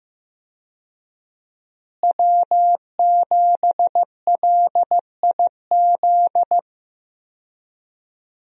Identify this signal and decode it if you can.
Morse code, heard as a keyed tone, about 15 wpm, 700 Hz. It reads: W7LIZ
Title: USA Call Signs - 5 Characters - Repeated 2x - 15wpm